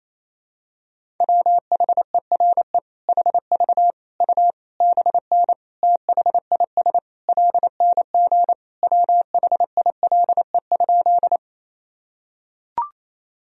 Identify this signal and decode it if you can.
Morse code, heard as a keyed tone, about 27 wpm, 700 Hz. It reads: WHERE HV U BN THIS LNG WHILE?
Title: Sentences from Top 300 Words and Abbreviations 28wpm